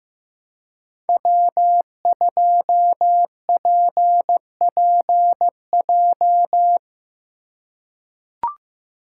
Morse code, heard as a keyed tone, about 15 wpm, 700 Hz. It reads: W2PPJ
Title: USA Call Signs - 4x Character Spacing - 15wpm